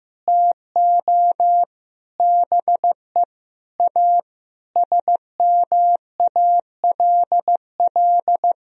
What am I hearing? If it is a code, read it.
Morse code, heard as a keyed tone, about 15 wpm, 700 Hz. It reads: TO BE A SMALL
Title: Sentences from Top 1000 Words - Review Part 1 - 15wpm